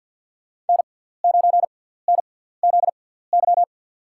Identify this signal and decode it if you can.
Morse code, heard as a keyed tone, about 47 wpm, spaced as for 17 wpm, 690 Hz. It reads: N9NZY
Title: USA Call Signs - 5 Characters - 6x Character Spacing - 50wpm